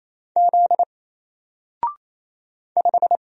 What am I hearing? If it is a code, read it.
Morse code, heard as a keyed tone, about 27 wpm, 700 Hz. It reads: Z 5
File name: Single-Letter-Number-28wpm